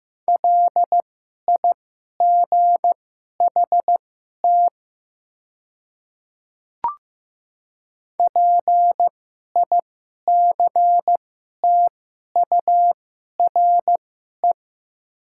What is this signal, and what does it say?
Morse code, heard as a keyed tone, about 15 wpm, spaced as for 11 wpm, 700 Hz. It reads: LIGHT PICTURE
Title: 2x Character Spacing - Top 200 Words - 15wpm